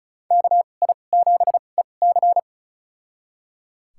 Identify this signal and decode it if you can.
Morse code, heard as a keyed tone, about 34 wpm, spaced as for 25 wpm, 700 Hz. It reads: KI7EC
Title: USA Call Signs - 5 Characters - 2x Character Spacing - 35wpm